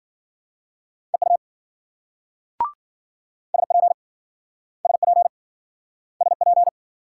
Morse code, heard as a keyed tone, about 53 wpm, 700 Hz. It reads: EU SP SP SP